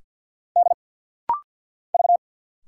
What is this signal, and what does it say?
Morse code, heard as a keyed tone, about 50 wpm, 700 Hz. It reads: D V
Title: Single Letters - Rapid-Fire 50wpm